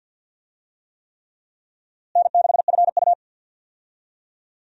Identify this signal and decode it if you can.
Morse code, heard as a keyed tone, about 47 wpm, 700 Hz. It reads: N6FU
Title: USA Call Signs - 4 Characters - Repeated 3x - 50wpm